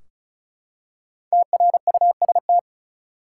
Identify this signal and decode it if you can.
Morse code, heard as a keyed tone, about 34 wpm, 700 Hz. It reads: TRUST
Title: Five Letter Words - 35wpm